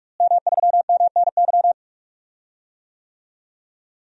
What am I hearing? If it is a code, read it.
Morse code, heard as a keyed tone, about 42 wpm, 700 Hz. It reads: M3MNY
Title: DXCC-100 Call Signs - 5 Characters - Repeated 3x - 45wpm